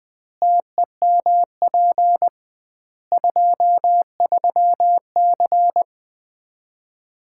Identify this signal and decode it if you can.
Morse code, heard as a keyed tone, about 20 wpm, 700 Hz. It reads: TEMP 23C
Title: Common QSO Phrases - 2x Word-Spacing - 20wpm